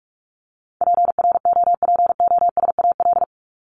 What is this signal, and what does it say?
Morse code, heard as a keyed tone, about 44 wpm, 700 Hz. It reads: PROPOSAL
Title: Top 1751-2000 Words 45wpm